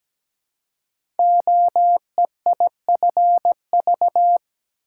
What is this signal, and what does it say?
Morse code, heard as a keyed tone, about 17 wpm, 700 Hz. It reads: OEIFV
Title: Code Groups - 5 Letters - 17wpm